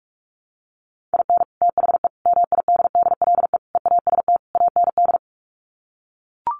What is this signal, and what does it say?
Morse code, heard as a keyed tone, about 44 wpm, 700 Hz. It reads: IN THE MIDDLE EAST AND
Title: Sets of 5 Words Encore 45wpm